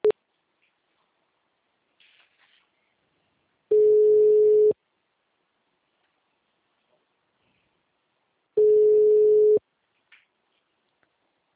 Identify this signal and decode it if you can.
Ringback tone (Europe, 425 Hz).